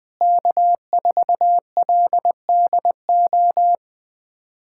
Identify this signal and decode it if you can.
Morse code, heard as a keyed tone, about 20 wpm, 700 Hz. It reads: K4LDO
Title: USA Callsign DE Callsign - 5 Characters - 20wpm